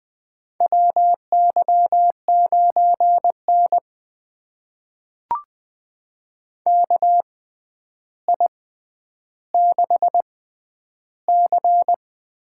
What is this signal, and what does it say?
Morse code, heard as a keyed tone, about 20 wpm, spaced as for 10 wpm, 700 Hz. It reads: WY9N KI6C
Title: USA Call Signs - 6x Character Spacing - 20wpm